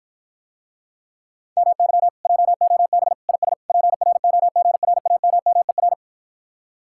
Morse code, heard as a keyed tone, about 54 wpm, 700 Hz. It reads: MY JOB IS PROGRAMMER